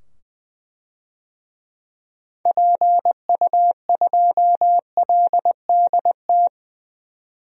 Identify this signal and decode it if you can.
Morse code, heard as a keyed tone, about 20 wpm, 700 Hz. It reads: PU2LDT
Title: DXCC-100 Call Signs - 6 Characters - 20wpm